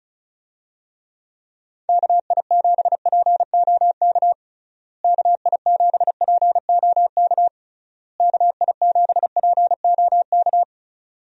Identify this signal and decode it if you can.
Morse code, heard as a keyed tone, about 36 wpm, 700 Hz. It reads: KI7POK KI7POK KI7POK
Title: USA Call Signs - 6 Characters - Repeated 3x - 35wpm